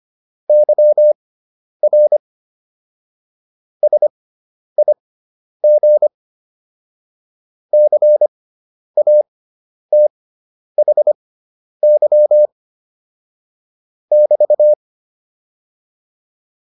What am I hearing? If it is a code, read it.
Morse code, heard as a keyed tone, about 25 wpm, spaced as for 10 wpm, 600 Hz. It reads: YR SIG CATHY <BT>